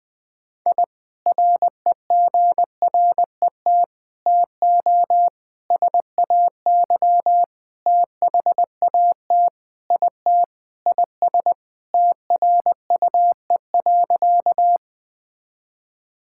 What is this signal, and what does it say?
Morse code, heard as a keyed tone, about 21 wpm, 700 Hz. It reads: I REGRET TO SAY THAT IT IS TRUE.
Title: Sentences with 3-Word Phrases - 20wpm